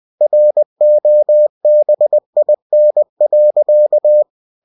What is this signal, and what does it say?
Morse code, heard as a keyed tone, about 20 wpm, 600 Hz. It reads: ROBIN.
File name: cw-books / winnie-the-pooh / winnie-20wpm-page-0088